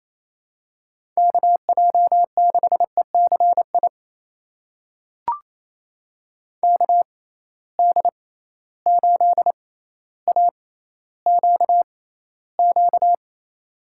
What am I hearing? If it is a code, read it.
Morse code, heard as a keyed tone, about 28 wpm, spaced as for 14 wpm, 700 Hz. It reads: KJ6ECI KD8AQQ
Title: USA Call Signs - 6 Characters - 6x Character Spacing - 28wpm